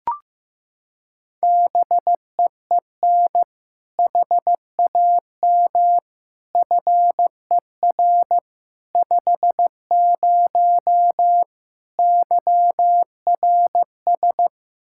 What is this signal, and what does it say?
Morse code, heard as a keyed tone, about 16 wpm, 700 Hz. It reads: BEEN HAM FER 50 YRS
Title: Practice QSO - Exchange 3 - More Info - 15wpm